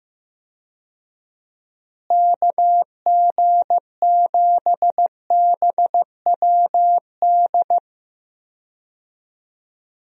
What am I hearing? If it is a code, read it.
Morse code, heard as a keyed tone, about 15 wpm, 700 Hz. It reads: KG7BWD
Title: USA Call Signs - 6 Characters - Repeated 3x - 15wpm